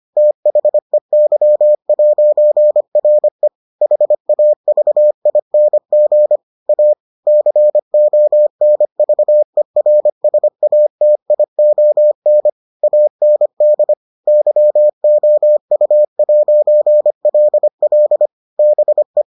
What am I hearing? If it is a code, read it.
Morse code, heard as a keyed tone, about 25 wpm, 600 Hz. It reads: THEY'RE HAVING A CONVERSATION AND YOU'LL BE